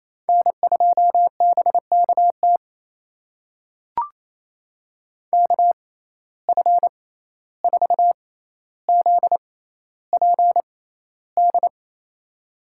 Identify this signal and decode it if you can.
Morse code, heard as a keyed tone, about 28 wpm, spaced as for 14 wpm, 700 Hz. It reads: N2BKT KF4ZPD
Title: USA Call Signs - 6x Character Spacing - 28wpm